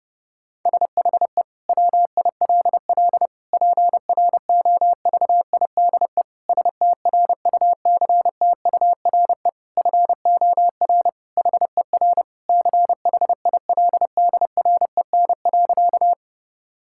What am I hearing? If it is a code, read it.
Morse code, heard as a keyed tone, about 30 wpm, 700 Hz. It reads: SHE WILL PROVIDE STRUCTURE FOR HER CHILDREN.
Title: Sentences from Top 1000 Words - Review Part 6 - 30wpm